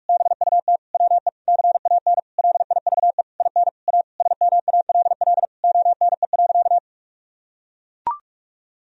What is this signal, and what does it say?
Morse code, heard as a keyed tone, about 41 wpm, 700 Hz. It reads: BUT WE CAN LIVE IN A SMALL ONE.
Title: Sentences from Top 1000 Words - Review Part 1 - 45wpm